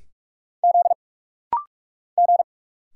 Morse code, heard as a keyed tone, about 44 wpm, 700 Hz. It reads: Z G
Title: Single Letter-Number Alt - Rapid-Fire 45wpm